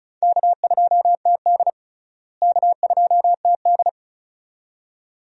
Morse code, heard as a keyed tone, about 36 wpm, 700 Hz. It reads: K2TD K2TD
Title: USA Call Signs - 4 Characters - Repeated 2x - 35wpm